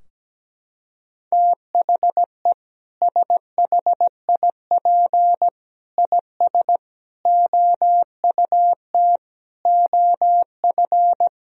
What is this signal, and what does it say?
Morse code, heard as a keyed tone, about 17 wpm, 700 Hz. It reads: THE SHIP IS OUT OF
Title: Sentences from Top 1000 Words - Review Part 4 - 17wpm